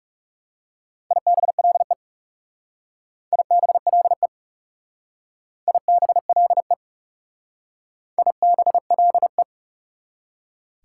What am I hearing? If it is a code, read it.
Morse code, heard as a keyed tone, about 38 wpm, 700 Hz. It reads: IBLE IBLE IBLE IBLE
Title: Speed-Racing Prefixes and Suffixes 30wpm